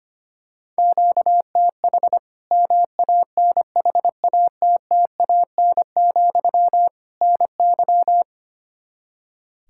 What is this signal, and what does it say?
Morse code, heard as a keyed tone, about 25 wpm, 700 Hz. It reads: QTH MANHATTAN, NY
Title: Practice QSO - Exchange 1 - Beginning - 25wpm